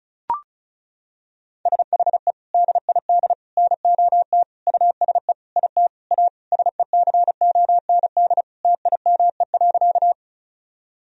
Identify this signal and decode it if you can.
Morse code, heard as a keyed tone, about 36 wpm, 700 Hz. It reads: SHE DID NOT USE IT A SECOND TIME.
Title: Sentences from Top 1000 Words - Review Part 1 - 35wpm